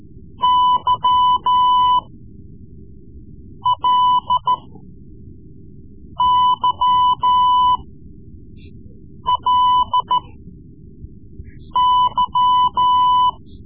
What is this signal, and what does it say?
Morse code, heard as a keyed tone, about 14 wpm, spaced as for 5 wpm, 1000 Hz. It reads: YLYLY
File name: Morse Lessons / Morse Lessons / Tape-3B